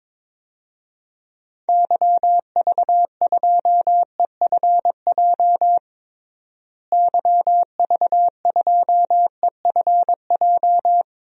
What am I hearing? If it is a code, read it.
Morse code, heard as a keyed tone, about 21 wpm, 700 Hz. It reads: YV2EFJ YV2EFJ
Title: DXCC-100 Call Signs - 6 Characters - Repeated 2x - 22wpm